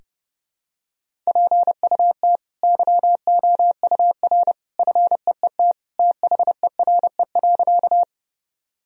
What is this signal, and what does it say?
Morse code, heard as a keyed tone, about 30 wpm, 700 Hz. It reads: PUT YOUR FEET THERE.
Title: Sentences from Top 1000 Words - Review Part 1 - 30wpm